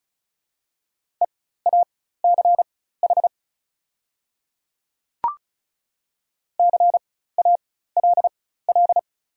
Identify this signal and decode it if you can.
Morse code, heard as a keyed tone, about 34 wpm, spaced as for 16 wpm, 690 Hz. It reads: EACH CALL
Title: 4x Character Spacing - Top 200 Words - 35wpm